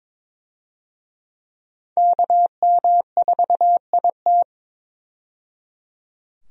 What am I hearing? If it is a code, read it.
Morse code, heard as a keyed tone, about 22 wpm, 700 Hz. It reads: KM4IT
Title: USA Call Signs - 5 Characters - 22wpm